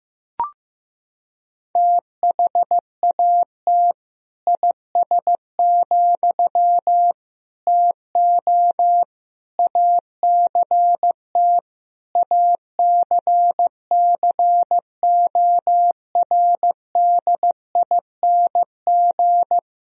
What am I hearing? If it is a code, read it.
Morse code, heard as a keyed tone, about 15 wpm, 700 Hz. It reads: THAT IS, TO ACT ACCORDING
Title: Sentences from Top 1000 Words - Review Part 4 - 15wpm